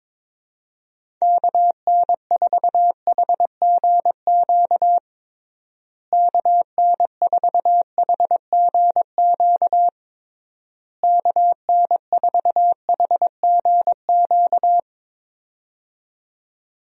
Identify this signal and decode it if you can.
Morse code, heard as a keyed tone, about 22 wpm, 700 Hz. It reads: KN4HGQ KN4HGQ KN4HGQ
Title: USA Call Signs - 6 Characters - Repeated 3x - 22wpm